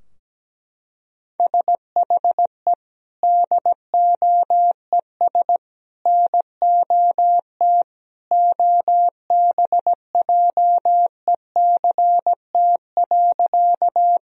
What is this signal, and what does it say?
Morse code, heard as a keyed tone, about 17 wpm, 700 Hz. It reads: SHE DOES NOT OBJECT.
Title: Sentences from Top 1000 Words - Review Part 3 - 17wpm